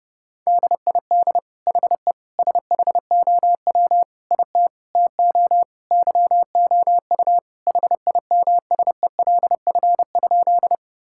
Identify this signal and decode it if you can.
Morse code, heard as a keyed tone, about 32 wpm, 700 Hz. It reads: DID HE SHOW IT TO YOU HIMSELF?
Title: Sentences from Top 1600 Words 30wpm